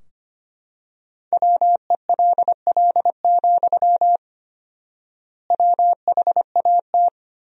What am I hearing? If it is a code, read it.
Morse code, heard as a keyed tone, about 25 wpm, 700 Hz. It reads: WELL, WHAT
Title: 4x Word-Spacing - Sentences from Top 500 Words - 25wpm